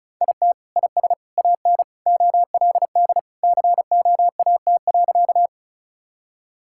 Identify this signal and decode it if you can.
Morse code, heard as a keyed tone, about 34 wpm, 700 Hz. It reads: IT IS AN OLD COAT.
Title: Sentences from Top 1600 Words 35wpm